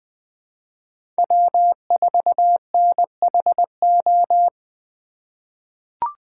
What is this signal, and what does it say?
Morse code, heard as a keyed tone, about 20 wpm, 700 Hz. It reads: W4NHO
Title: USA Call Signs - 5 Characters - 2x Character Spacing - 20wpm